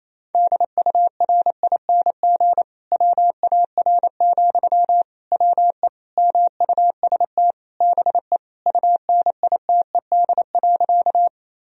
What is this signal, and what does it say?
Morse code, heard as a keyed tone, about 28 wpm, 700 Hz. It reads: DURING WAR, WE MUST BE UNITED.
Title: Sentences from Top 1000 Words - Review Part 4 - 28wpm